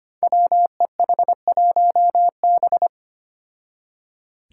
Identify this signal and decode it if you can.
Morse code, heard as a keyed tone, about 25 wpm, 700 Hz. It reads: WEH1B
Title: Five Challenging Letter-Number Combinations 25wpm